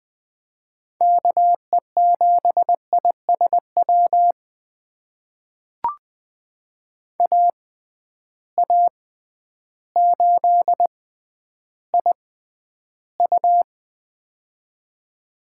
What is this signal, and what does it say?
Morse code, heard as a keyed tone, about 20 wpm, spaced as for 11 wpm, 700 Hz. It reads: KE7ISW AA8IU
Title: USA Call Signs - 6x Character Spacing - 20wpm